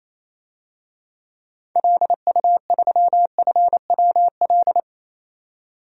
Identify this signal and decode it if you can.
Morse code, heard as a keyed tone, about 27 wpm, 700 Hz. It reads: LU3FWL
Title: DXCC-100 Call Signs - 6 Characters - 28wpm